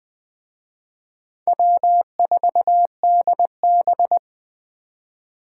Morse code, heard as a keyed tone, about 21 wpm, 700 Hz. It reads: W4DB